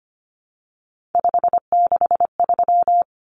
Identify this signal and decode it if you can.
Morse code, heard as a keyed tone, about 25 wpm, 700 Hz. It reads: 563
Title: Common Cut Numbers - 3-Digit 25wpm